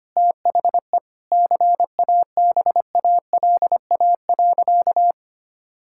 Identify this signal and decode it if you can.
Morse code, heard as a keyed tone, about 25 wpm, 700 Hz. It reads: THE CABALA.
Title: Pangram Challenge v2 - 25wpm